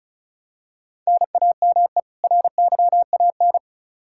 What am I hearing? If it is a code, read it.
Morse code, heard as a keyed tone, about 36 wpm, 700 Hz. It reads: NAME RYAN
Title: K1USN SST QSO Practice 35wpm